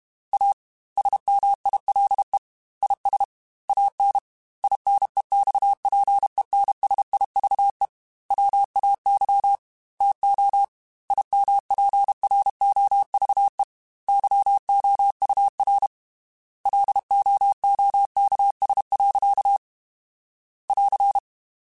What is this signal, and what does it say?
Morse code, heard as a keyed tone, about 32 wpm, 800 Hz. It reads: A SMILE IS AN INEXPENSIVE WAY TO IMPROVE YOUR LOOKS. <AR>
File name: Quotes / Quotes-mp3s / A Smile @30